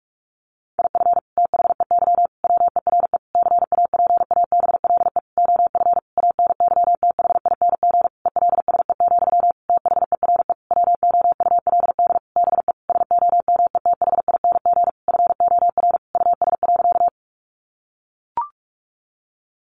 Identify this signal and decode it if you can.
Morse code, heard as a keyed tone, about 44 wpm, 700 Hz. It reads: IF THEY WERE CAPABLE OF ANYTHING ELSE, THERE WOULD BE SOMETHING FOR US.
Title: Sentences from Top 1200 Words 45wpm